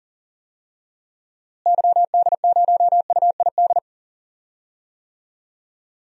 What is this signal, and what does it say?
Morse code, heard as a keyed tone, about 39 wpm, 700 Hz. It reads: YD0UID
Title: DXCC-100 Call Signs - 6 Characters - Repeated 3x - 40wpm